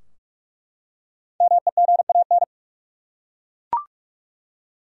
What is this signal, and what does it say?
Morse code, heard as a keyed tone, about 43 wpm, 700 Hz. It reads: MEGAN
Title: USA Common Names - 45wpm